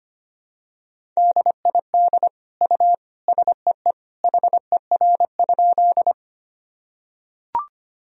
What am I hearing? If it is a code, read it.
Morse code, heard as a keyed tone, about 25 wpm, 700 Hz. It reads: DID U SEE HER?
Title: Sentences from Top 100 Words and Abbreviations 25wpm